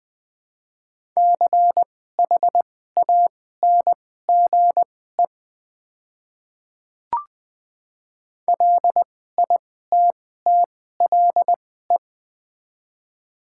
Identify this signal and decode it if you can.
Morse code, heard as a keyed tone, about 20 wpm, spaced as for 15 wpm, 700 Hz. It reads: CHANGE LITTLE
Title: 2x Character Spacing - Top 200 Words - 20wpm